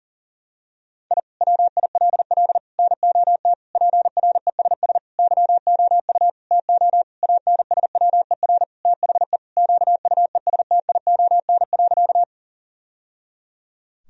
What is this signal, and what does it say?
Morse code, heard as a keyed tone, about 40 wpm, 700 Hz. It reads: I WILL NOT PRESS YOU TO ANSWER THE QUESTION.
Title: Sentences from Top 1000 Words - Review Part 1 - 40wpm